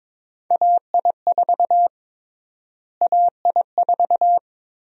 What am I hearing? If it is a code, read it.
Morse code, heard as a keyed tone, about 22 wpm, 700 Hz. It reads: AI4 AI4
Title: USA Call Sign Prefixes - Repeated 2x - 22wpm